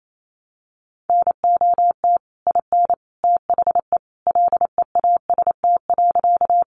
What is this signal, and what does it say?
Morse code, heard as a keyed tone, about 28 wpm, 700 Hz. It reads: NOT IN THE LEAST.
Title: Sentences from Top 700 Words 28wpm